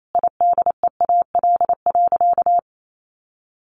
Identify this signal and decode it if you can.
Morse code, heard as a keyed tone, about 28 wpm, 700 Hz. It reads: IDEAL.
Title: Sentences from Top 1400 Words 28wpm